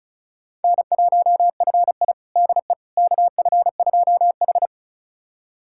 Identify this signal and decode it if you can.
Morse code, heard as a keyed tone, about 34 wpm, 700 Hz. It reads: N1FI DE KF2H
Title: USA Callsign DE Callsign - 4 Characters - 35wpm